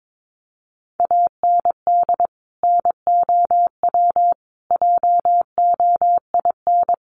Morse code, heard as a keyed tone, about 22 wpm, 700 Hz. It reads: AND NOW JOIN
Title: Sentences from Top 1400 Words 22wpm